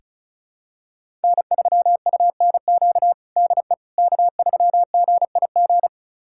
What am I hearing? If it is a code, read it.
Morse code, heard as a keyed tone, about 36 wpm, 700 Hz. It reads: N3UNQ DE K3GIG
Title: USA Callsign DE Callsign - 5 Characters - 35wpm